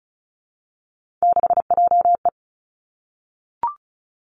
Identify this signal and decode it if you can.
Morse code, heard as a keyed tone, about 36 wpm, 700 Hz. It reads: 6JE
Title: Three Challenging Letter-Number Combinations 35wpm